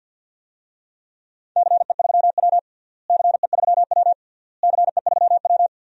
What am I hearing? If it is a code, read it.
Morse code, heard as a keyed tone, about 51 wpm, 700 Hz. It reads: CE3W CE3W CE3W
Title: DXCC-100 Call Signs - 4 Characters - Repeated 3x - 50wpm